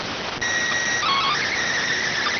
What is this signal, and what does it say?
SSTV (slow-scan television), PD 290.